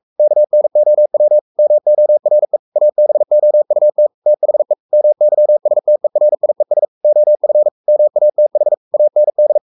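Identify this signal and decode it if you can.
Morse code, heard as a keyed tone, about 44 wpm, 600 Hz. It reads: KNOW MORE ABOUT THE MYSTERIES OF MATH AND
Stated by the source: Morse code at 43 wpm sending Ken Follett's novel Never in English, from CW med Gnister.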